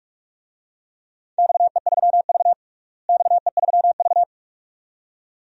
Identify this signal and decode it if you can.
Morse code, heard as a keyed tone, about 46 wpm, 700 Hz. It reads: XE3V XE3V
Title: DXCC-100 Call Signs - 4 Characters - Repeated 2x - 45wpm